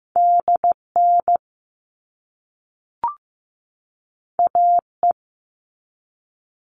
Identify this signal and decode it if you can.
Morse code, heard as a keyed tone, about 15 wpm, 700 Hz. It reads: DN AE
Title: All Cut Numbers - 2-Digit 15wpm